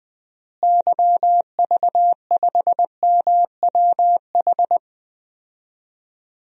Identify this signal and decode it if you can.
Morse code, heard as a keyed tone, about 20 wpm, 700 Hz. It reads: YV5MWH
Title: DXCC-100 Call Signs - 6 Characters - Repeated 2x - 20wpm